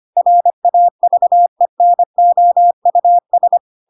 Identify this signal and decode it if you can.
Morse code, heard as a keyed tone, about 25 wpm, 700 Hz. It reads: RAVENOUS
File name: Pinocchio / Pinocchio ch5 25 25 ik1hge